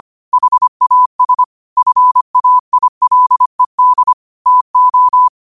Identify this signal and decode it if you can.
Morse code, heard as a keyed tone, about 25 wpm, 1000 Hz. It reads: HAS FAILED TO
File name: qotdfiles / 25wpm